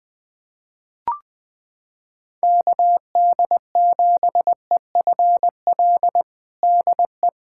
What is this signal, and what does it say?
Morse code, heard as a keyed tone, about 21 wpm, 700 Hz. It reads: KD7EFL DE
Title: USA Callsign DE Callsign - 6 Characters - 20wpm